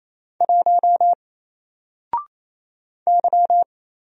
Morse code, heard as a keyed tone, about 27 wpm, 700 Hz. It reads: 1 Y
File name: Single-Letter-Number-28wpm